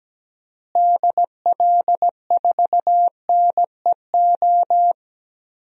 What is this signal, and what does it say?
Morse code, heard as a keyed tone, about 17 wpm, 700 Hz. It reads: DL4NEO
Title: DXCC-100 Call Signs - 6 Characters - Repeated 2x - 17wpm